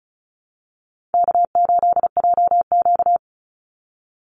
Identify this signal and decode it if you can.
Morse code, heard as a keyed tone, about 34 wpm, 700 Hz. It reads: K8JQ
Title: Call Signs Repeated 3x - 35wpm